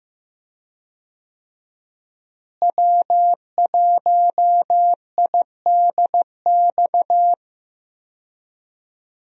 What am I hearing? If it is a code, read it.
Morse code, heard as a keyed tone, about 16 wpm, 700 Hz. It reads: W1IDX
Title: USA Call Signs - 5 Characters - Repeated 2x - 15wpm